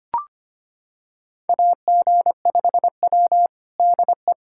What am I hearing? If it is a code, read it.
Morse code, heard as a keyed tone, about 25 wpm, 700 Hz. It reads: AG5W DE
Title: USA Callsign DE Callsign - 4 Characters - 25wpm